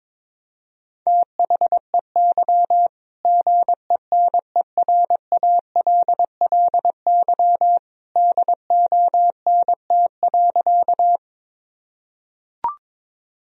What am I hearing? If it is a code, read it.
Morse code, heard as a keyed tone, about 22 wpm, 700 Hz. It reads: THEY GENERALLY DONT.
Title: Sentences from Top 1000 Words - Review Part 4 - 22wpm